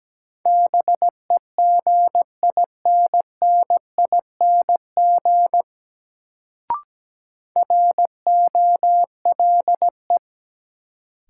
Morse code, heard as a keyed tone, about 17 wpm, 700 Hz. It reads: BEGINNING ROLE
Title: Sets of 3 Words - New Word Drill - 17wpm